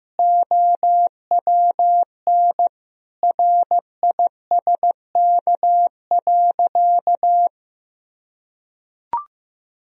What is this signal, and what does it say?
Morse code, heard as a keyed tone, about 15 wpm, 700 Hz. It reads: OWN RISK.
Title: Sentences from Top 1000 Words - Review Part 4 - 15wpm